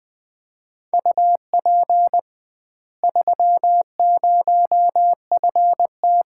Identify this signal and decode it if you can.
Morse code, heard as a keyed tone, about 20 wpm, 700 Hz. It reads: UP 30FT
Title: Common QSO Phrases - 2x Word-Spacing - 20wpm